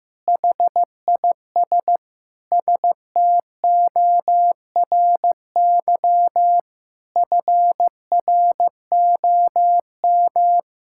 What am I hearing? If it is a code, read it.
Morse code, heard as a keyed tone, about 15 wpm, 700 Hz. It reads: HIS STORY FROM
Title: Sentences from Top 1000 Words - Review Part 5 - 15wpm